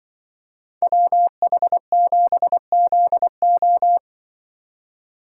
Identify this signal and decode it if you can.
Morse code, heard as a keyed tone, about 25 wpm, 700 Hz. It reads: WH7ZO
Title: Speed Racing - USA 2x2 Call Signs 20wpm